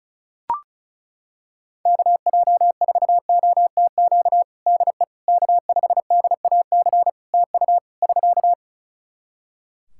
Morse code, heard as a keyed tone, about 34 wpm, 700 Hz. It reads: KJ4OTQ DE K5DAC TU <SK>
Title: Practice QSO - Exchange 4 - Ending - 35wpm